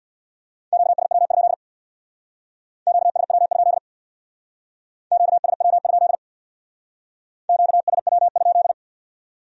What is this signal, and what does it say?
Morse code, heard as a keyed tone, about 61 wpm, 700 Hz. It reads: QSW? QSW? QSW? QSW?
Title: Speed-Racing Q-Codes - 50wpm